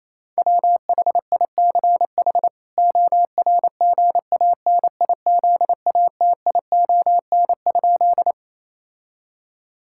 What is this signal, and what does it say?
Morse code, heard as a keyed tone, about 28 wpm, 700 Hz. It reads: WHICH ORGANIZATION?
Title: Sentences from Top 1000 Words - Review Part 5 - 28wpm